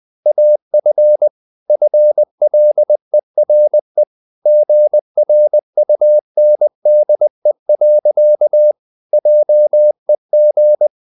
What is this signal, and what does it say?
Morse code, heard as a keyed tone, about 20 wpm, 600 Hz. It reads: AF FLERE GRUNDE. JEG
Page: Ferie 20 wpm